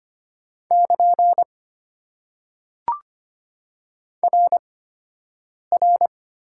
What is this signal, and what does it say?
Morse code, heard as a keyed tone, about 25 wpm, spaced as for 7 wpm, 690 Hz. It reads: <KN> RR
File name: Top 90 QSO Elements - 8x Character Spacing - 25wpm